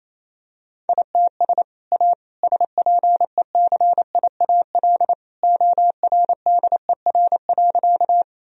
Morse code, heard as a keyed tone, about 28 wpm, 700 Hz. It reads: ITS A SPECIAL ORDER.
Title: Sentences from Top 1000 Words - Review Part 3 - 28wpm